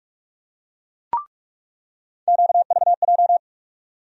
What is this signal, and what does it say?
Morse code, heard as a keyed tone, about 46 wpm, 700 Hz. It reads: QVJ
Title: USA Call Sign Postfixes - 3 Letters 45wpm